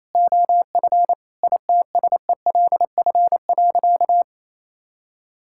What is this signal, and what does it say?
Morse code, heard as a keyed tone, about 28 wpm, 700 Hz. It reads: OF ITSELF.
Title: Sentences from Top 1000 Words - Review Part 5 - 28wpm